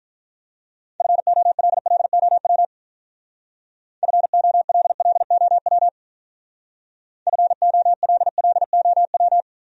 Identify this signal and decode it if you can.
Morse code, heard as a keyed tone, about 41 wpm, 700 Hz. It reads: FOLLOW FOLLOW FOLLOW
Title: Speed-Racing Top 500 Words 35wpm